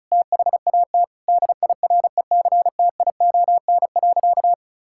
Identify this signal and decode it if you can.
Morse code, heard as a keyed tone, about 36 wpm, 700 Hz. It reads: THAT DIRECTION.
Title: Sentences from Top 1000 Words - Review Part 6 - 35wpm